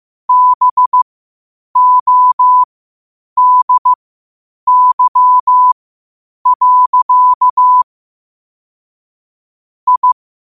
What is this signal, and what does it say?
Morse code, heard as a keyed tone, about 15 wpm, spaced as for 8 wpm, 1000 Hz. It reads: BODY. I
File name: qotdfiles / 8wpm